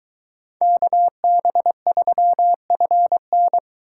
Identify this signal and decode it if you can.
Morse code, heard as a keyed tone, about 23 wpm, 700 Hz. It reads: KB3FN